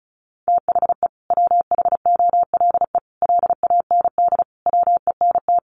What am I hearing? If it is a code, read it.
Morse code, heard as a keyed tone, about 34 wpm, 700 Hz. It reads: THE WHOLE LAND WENT
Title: Sentences from Top 1400 Words 35wpm